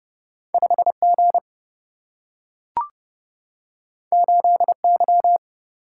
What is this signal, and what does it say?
Morse code, heard as a keyed tone, about 30 wpm, 700 Hz. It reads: 5G 8Y
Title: Single-Letter Single-Number 30wpm